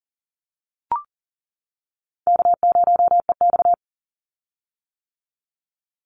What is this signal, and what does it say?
Morse code, heard as a keyed tone, about 38 wpm, 700 Hz. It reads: K0EX
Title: Call Signs - 40wpm